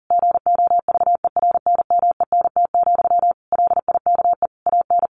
Morse code, heard as a keyed tone, about 40 wpm, 700 Hz. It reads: GOVERNMENT, LIKE AN
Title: Sentences from Top 1400 Words 40wpm